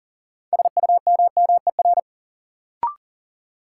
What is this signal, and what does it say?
Morse code, heard as a keyed tone, about 40 wpm, 700 Hz. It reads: SUMMER
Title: Sentences from Top 1100 Words - New Word Drill Alt - 40wpm